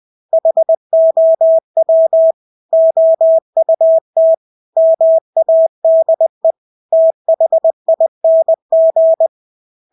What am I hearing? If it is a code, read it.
Morse code, heard as a keyed tone, about 20 wpm, 650 Hz. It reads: HOW OUT MADE THING
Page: Engelske ord 20 wpm